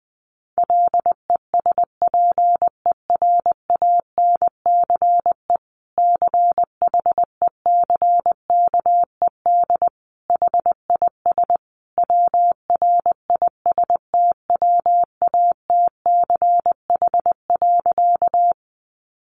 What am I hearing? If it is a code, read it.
Morse code, heard as a keyed tone, about 20 wpm, 700 Hz. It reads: LESPERANCE CHECKED HIS WRISTWATCH.
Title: A Sound Of Thunder 20wpm